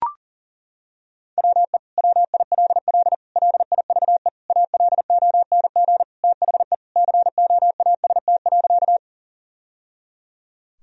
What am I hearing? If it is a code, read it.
Morse code, heard as a keyed tone, about 37 wpm, 700 Hz. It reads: WE WILL LIVE ALONG THE COAST.
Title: Sentences from Top 1000 Words - Review Part 4 - 40wpm